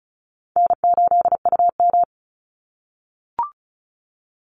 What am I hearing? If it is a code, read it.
Morse code, heard as a keyed tone, about 34 wpm, 700 Hz. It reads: N8UM
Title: Call Signs - 35wpm